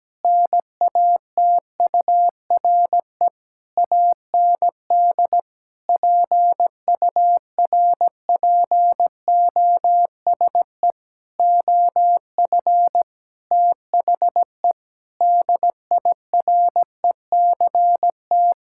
Morse code, heard as a keyed tone, about 17 wpm, 700 Hz. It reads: NATURE AND PURPOSE OF THE DIRECT
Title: Sentences from Top 1000 Words - Review Part 5 - 17wpm